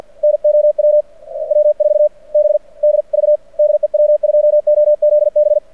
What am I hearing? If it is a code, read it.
Morse code, heard as a keyed tone, about 48 wpm, spaced as for 40 wpm, 600 Hz. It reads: NOW W4BDV DE W2OZB
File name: archived / CRSnet